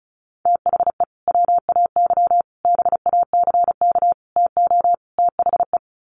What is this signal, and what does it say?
Morse code, heard as a keyed tone, about 34 wpm, 700 Hz. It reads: THE WAY BACK TO THE
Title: Sets of 5 Words Encore2 35wpm